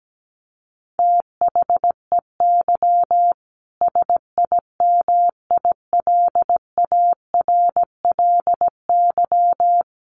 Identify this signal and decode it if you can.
Morse code, heard as a keyed tone, about 17 wpm, 700 Hz. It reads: THEY SIMILARLY
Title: Sentences from Top 1400 Words 17wpm